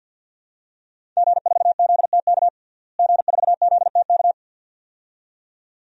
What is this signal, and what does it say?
Morse code, heard as a keyed tone, about 52 wpm, 700 Hz. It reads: G4ZTK G4ZTK
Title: DXCC-100 Call Signs - 5 Characters - Repeated 2x - 50wpm